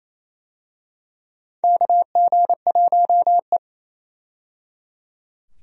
Morse code, heard as a keyed tone, about 28 wpm, 700 Hz. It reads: KG1E
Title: USA Call Signs - 4 Characters - 28wpm